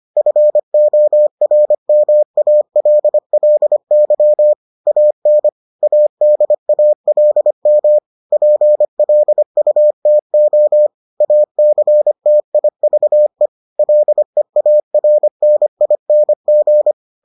Morse code, heard as a keyed tone, about 25 wpm, 600 Hz. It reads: FORMALLY AN ADALM PLUTO ACTIVE LEARNING